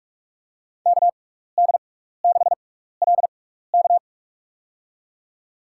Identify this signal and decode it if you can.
Morse code, heard as a keyed tone, about 46 wpm, spaced as for 16 wpm, 700 Hz. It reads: KD6LK
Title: USA Call Signs - 5 Characters - 6x Character Spacing - 45wpm